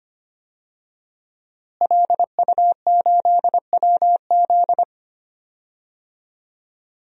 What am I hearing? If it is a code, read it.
Morse code, heard as a keyed tone, about 25 wpm, 700 Hz. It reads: LU8WZ
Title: DXCC-100 Call Signs - 5 Characters - Repeated 3x - 25wpm